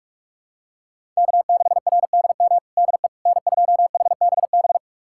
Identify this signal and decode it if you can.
Morse code, heard as a keyed tone, about 46 wpm, 700 Hz. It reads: K6RDM DE N2HBB
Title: USA Callsign DE Callsign - 5 Characters - 45wpm